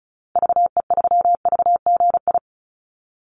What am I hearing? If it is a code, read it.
Morse code, heard as a keyed tone, about 34 wpm, 700 Hz. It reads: VE3VGI
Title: Call Signs Repeated 2x - 35wpm